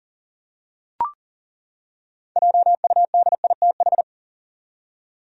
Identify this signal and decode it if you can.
Morse code, heard as a keyed tone, about 40 wpm, 700 Hz. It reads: JUDITH
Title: USA Common Names - 40wpm